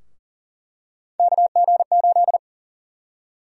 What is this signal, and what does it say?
Morse code, heard as a keyed tone, about 40 wpm, 700 Hz. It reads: KG8
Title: USA Call Sign Prefixes - Repeated 2x - 40wpm